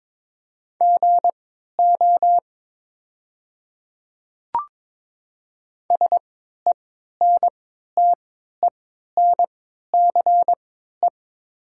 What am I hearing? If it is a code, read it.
Morse code, heard as a keyed tone, about 22 wpm, spaced as for 12 wpm, 690 Hz. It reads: GO SENTENCE
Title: 3x Character Spacing - Top 200 Words - 22wpm